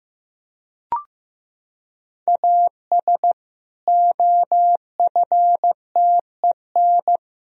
Morse code, heard as a keyed tone, about 15 wpm, 700 Hz. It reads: AS OFTEN
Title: Sentences from Top 1000 Words - Review Part 1 - 15wpm